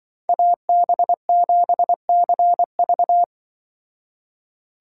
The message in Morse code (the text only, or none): AB7CV